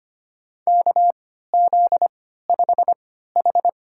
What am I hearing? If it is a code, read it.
Morse code, heard as a keyed tone, about 25 wpm, spaced as for 14 wpm, 700 Hz. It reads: KZ5H
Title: USA Call Signs - 4 Characters - 3x Character Spacing - 25wpm